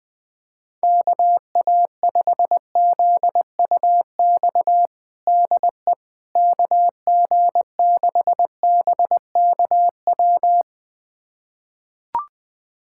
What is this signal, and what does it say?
Morse code, heard as a keyed tone, about 20 wpm, 700 Hz. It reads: KA5ZUX DE KG6BKW
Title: USA Callsign DE Callsign 20wpm